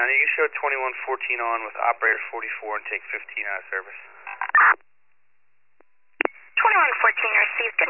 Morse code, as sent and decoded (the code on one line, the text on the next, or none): none